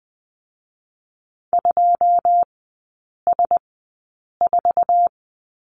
Morse code, heard as a keyed tone, about 20 wpm, spaced as for 8 wpm, 690 Hz. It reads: 2S4
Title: 159 Lesson 26e - Letter g - Three Characters 20wpm